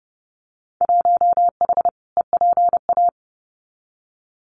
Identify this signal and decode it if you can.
Morse code, heard as a keyed tone, about 29 wpm, 700 Hz. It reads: 1H EPA
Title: Winter Field Day Exchanges 30wpm